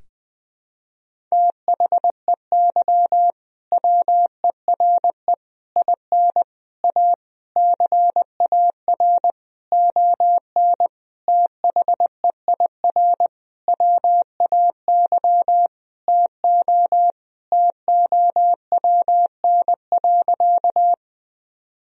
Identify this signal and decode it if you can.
Morse code, heard as a keyed tone, about 20 wpm, 700 Hz. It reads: THEY WERE IN A CAR ON THEIR WAY TO TOWN.
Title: Sentences from Top 1000 Words - Review Part 1 - 20wpm